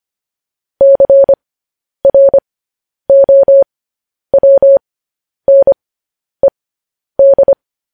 Morse code, heard as a keyed tone, about 25 wpm, spaced as for 10 wpm, 550 Hz. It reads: CROWNED
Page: news headlines in morse code at 10 wpm